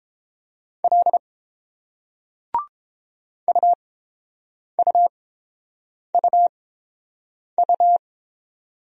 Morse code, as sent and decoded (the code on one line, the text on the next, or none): .-.. / ..- ..- ..- ..-
L UUUU